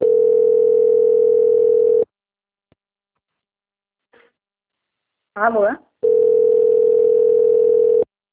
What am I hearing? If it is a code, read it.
Ringback tone (North America), 440 + 480 Hz, with speech.
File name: out- - - - - .4973